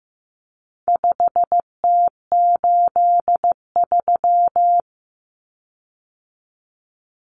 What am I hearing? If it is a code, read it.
Morse code, heard as a keyed tone, about 15 wpm, 700 Hz. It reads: 5T83
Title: Common Cut Numbers - 4-Digit 15wpm